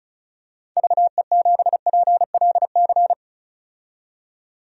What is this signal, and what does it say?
Morse code, heard as a keyed tone, about 34 wpm, 700 Hz. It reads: VE7PLC
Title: DXCC-100 Call Signs - 6 Characters - 35wpm